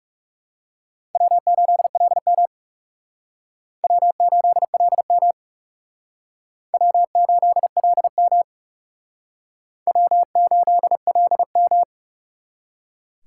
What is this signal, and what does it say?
Morse code, heard as a keyed tone, about 38 wpm, 700 Hz. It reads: W8LM W8LM W8LM W8LM
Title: Speed Racing - USA 1x2 Call Signs 30wpm